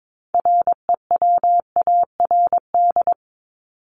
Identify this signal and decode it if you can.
Morse code, heard as a keyed tone, about 22 wpm, 700 Hz. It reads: REWARD
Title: Top 1751-2000 Words 22wpm